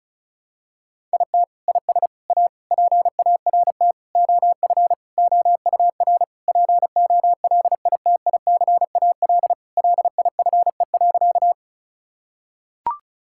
Morse code, heard as a keyed tone, about 36 wpm, 700 Hz. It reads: IT IS A PART OF OUR POLITICAL LIFE.
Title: Sentences from Top 1000 Words - Review Part 5 - 35wpm